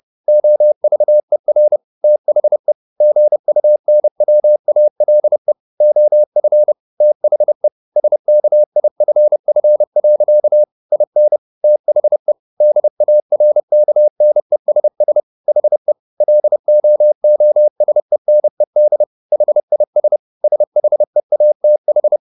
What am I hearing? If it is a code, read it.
Morse code, heard as a keyed tone, about 30 wpm, 600 Hz. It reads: OVER THE GUNWALE OF THE SKIFF. IN THE DARKNESS HE LOOSENED HIS SHEATH
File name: the old man and the sea / the old man and the sea 1 30wpm